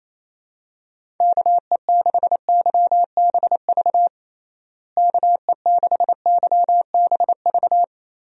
Morse code, heard as a keyed tone, about 27 wpm, 700 Hz. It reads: KE6YBV KE6YBV
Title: USA Call Signs - 6 Characters - Repeated 3x - 28wpm